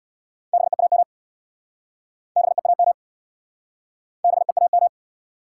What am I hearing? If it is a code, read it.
Morse code, heard as a keyed tone, about 69 wpm, 700 Hz. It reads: BEAM BEAM BEAM